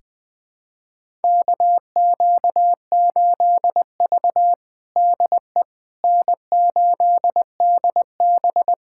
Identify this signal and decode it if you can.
Morse code, heard as a keyed tone, about 21 wpm, 700 Hz. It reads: KQ8V DE N8DB
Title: USA Callsign DE Callsign - 4 Characters - 20wpm